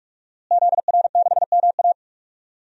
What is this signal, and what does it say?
Morse code, heard as a keyed tone, about 46 wpm, 700 Hz. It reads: ZR6MA
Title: DXCC-100 Call Signs - 5 Characters - Repeated 3x - 45wpm